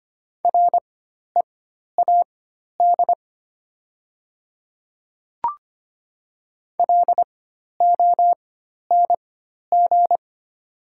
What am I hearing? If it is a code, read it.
Morse code, heard as a keyed tone, about 25 wpm, spaced as for 12 wpm, 700 Hz. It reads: READ LONG
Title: 4x Character Spacing - Top 200 Words - 25wpm